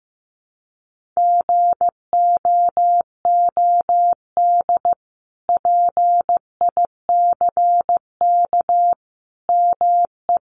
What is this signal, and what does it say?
Morse code, heard as a keyed tone, about 15 wpm, 700 Hz. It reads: GOOD PICK ME
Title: Sentences from Top 1000 Words 15wpm